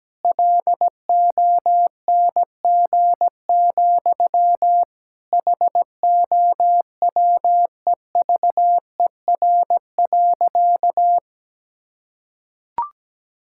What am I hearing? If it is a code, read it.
Morse code, heard as a keyed tone, about 17 wpm, 700 Hz. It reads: LONG, HOWEVER.
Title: Sentences from Top 1000 Words - Review Part 6 - 17wpm